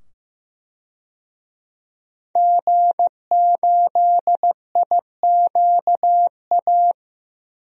Morse code, heard as a keyed tone, about 15 wpm, 700 Hz. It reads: G8IQA
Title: DXCC-100 Call Signs - 5 Characters - 15wpm